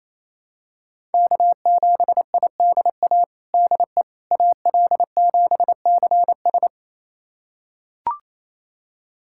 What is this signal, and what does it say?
Morse code, heard as a keyed tone, about 28 wpm, 700 Hz. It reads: K7IDA DE AL7CS
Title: USA Callsign DE Callsign - 5 Characters - 28wpm